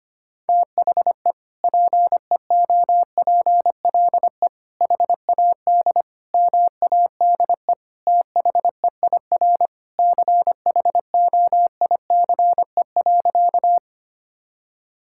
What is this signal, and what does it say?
Morse code, heard as a keyed tone, about 25 wpm, 700 Hz. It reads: THE PEOPLE HAD MADE THEIR CHOICE.
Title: Sentences from Top 1000 Words - Review Part 6 - 25wpm